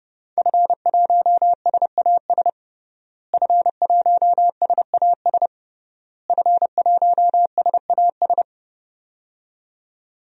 Morse code, heard as a keyed tone, about 30 wpm, 700 Hz. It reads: F1SAS F1SAS F1SAS
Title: DXCC-100 Call Signs - 5 Characters - Repeated 3x - 30wpm